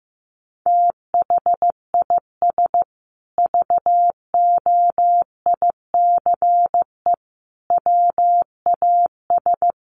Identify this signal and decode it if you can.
Morse code, heard as a keyed tone, about 15 wpm, 700 Hz. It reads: THIS VOICE WAS
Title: Sentences from Top 500 Words 15wpm